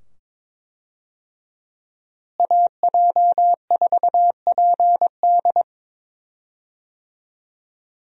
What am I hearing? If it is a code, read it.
Morse code, heard as a keyed tone, about 22 wpm, 700 Hz. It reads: AJ4PD
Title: USA Call Signs - 5 Characters - Repeated 2x - 22wpm